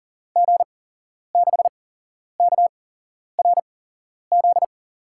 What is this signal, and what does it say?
Morse code, heard as a keyed tone, about 39 wpm, spaced as for 11 wpm, 700 Hz. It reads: G6KRZ
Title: USA Call Signs - 6 Characters - 8x Character Spacing - 40wpm